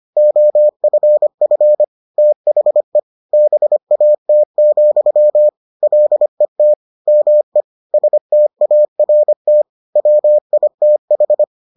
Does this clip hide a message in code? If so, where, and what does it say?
Yes: Morse code from 0.2 to 11.4 s, heard as a keyed tone, about 25 wpm, 600 Hz. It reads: OFF THE BAT, LET ME START WITH